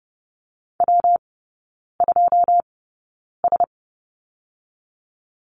Morse code, heard as a keyed tone, about 31 wpm, 700 Hz. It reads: W 2 S
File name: Lesson 19e - Letter w - Three Characters 30wpm